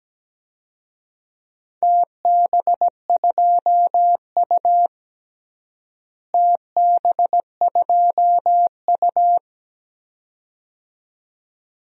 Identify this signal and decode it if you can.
Morse code, heard as a keyed tone, about 17 wpm, 700 Hz. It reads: TB2U TB2U
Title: DXCC-100 Call Signs - 4 Characters - Repeated 2x - 17wpm